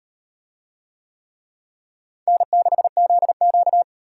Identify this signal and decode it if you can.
Morse code, heard as a keyed tone, about 38 wpm, 700 Hz. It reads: N6ZQ